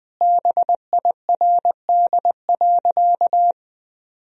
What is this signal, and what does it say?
Morse code, heard as a keyed tone, about 21 wpm, 700 Hz. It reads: BIRD.
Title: Sentences from Top 1000 Words - Review Part 2 - 20wpm